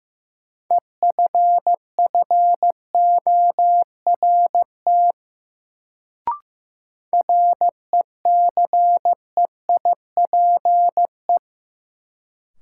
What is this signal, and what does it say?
Morse code, heard as a keyed tone, about 15 wpm, 700 Hz. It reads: EFFORT RECEIPE
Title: Sentences from Top 1200 Words - New Word Drill Alt - 15wpm